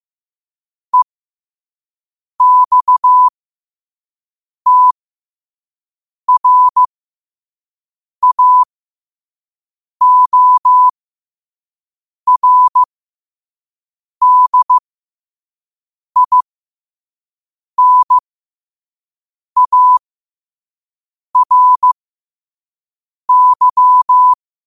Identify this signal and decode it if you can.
Morse code, heard as a keyed tone, about 15 wpm, spaced as for 5 wpm, 1000 Hz. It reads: EXTRAORDINARY
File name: qotdfiles / 5wpm